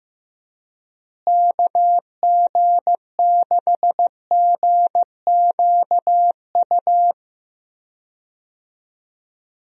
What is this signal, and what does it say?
Morse code, heard as a keyed tone, about 15 wpm, 700 Hz. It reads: KG6GQU
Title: USA Call Signs - 6 Characters - Repeated 3x - 15wpm